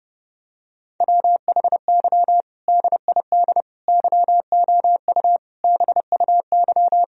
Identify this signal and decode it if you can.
Morse code, heard as a keyed tone, about 30 wpm, 700 Hz. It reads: WHY DID YOU BUY
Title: Sentences from Top 1000 Words - Review Part 4 - 30wpm